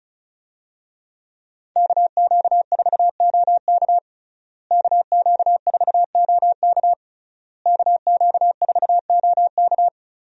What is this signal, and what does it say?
Morse code, heard as a keyed tone, about 36 wpm, 700 Hz. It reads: KQ4OK KQ4OK KQ4OK
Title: USA Call Signs - 5 Characters - Repeated 3x - 35wpm